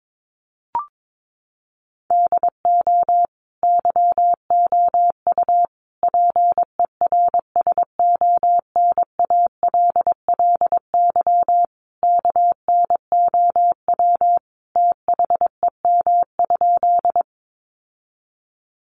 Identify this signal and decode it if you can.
Morse code, heard as a keyed tone, about 22 wpm, 700 Hz. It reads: DO YOU PERSONALLY KNOW THEM?
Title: Sentences from Top 1400 Words 22wpm